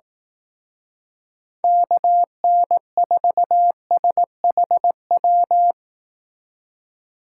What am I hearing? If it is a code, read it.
Morse code, heard as a keyed tone, about 18 wpm, 700 Hz. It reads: KN4SHW
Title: Speed Racing - USA 2x3 Call Signs 15wpm